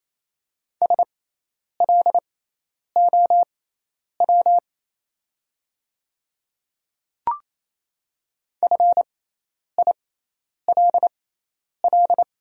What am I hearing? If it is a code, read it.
Morse code, heard as a keyed tone, about 27 wpm, spaced as for 10 wpm, 700 Hz. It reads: SLOW FILL
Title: 6x Character Spacing - Top 500 Words - 28wpm